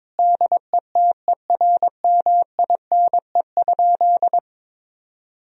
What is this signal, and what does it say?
Morse code, heard as a keyed tone, about 21 wpm, 700 Hz. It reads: DETERMINE?
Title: Sentences from Top 1000 Words - Review Part 4 - 22wpm